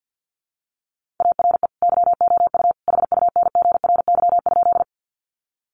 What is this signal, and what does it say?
Morse code, heard as a keyed tone, about 50 wpm, 700 Hz. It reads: ARE YOU HUNGRY?
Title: Sentences from Top 1400 Words 50wpm